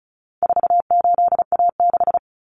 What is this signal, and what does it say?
Morse code, heard as a keyed tone, about 34 wpm, 700 Hz. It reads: 48A6
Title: Common Cut Numbers - 4-Digit 35wpm